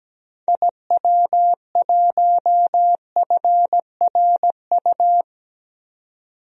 Morse code, heard as a keyed tone, about 17 wpm, 700 Hz. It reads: IW1FRU
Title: DXCC-100 Call Signs - 6 Characters - 17wpm